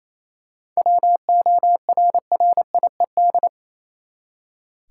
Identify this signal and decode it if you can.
Morse code, heard as a keyed tone, about 28 wpm, 700 Hz. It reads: WORRIED
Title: Seven Letter Words - 28wpm